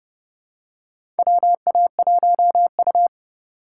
Morse code, heard as a keyed tone, about 32 wpm, 700 Hz. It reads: WA1U
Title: USA Call Signs - 4 Characters - 6x Character Spacing - 30wpm